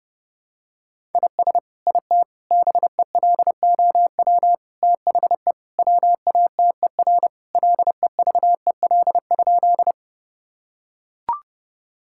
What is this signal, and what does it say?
Morse code, heard as a keyed tone, about 30 wpm, 700 Hz. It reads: IS IT BELOW THE WATER LEVEL?
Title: Sentences from Top 1000 Words - Review Part 5 - 30wpm